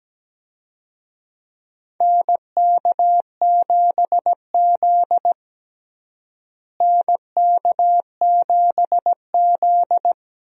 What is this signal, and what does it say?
Morse code, heard as a keyed tone, about 17 wpm, 700 Hz. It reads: NK7Z NK7Z
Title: USA Call Signs - 4 Characters - Repeated 3x - 17wpm